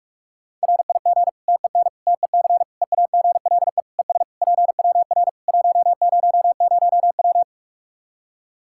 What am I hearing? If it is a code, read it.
Morse code, heard as a keyed tone, about 43 wpm, 700 Hz. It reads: RIG TEN TEC EAGLE ES PWR 100W
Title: Practice QSO - Exchange 2 - Info - 45wpm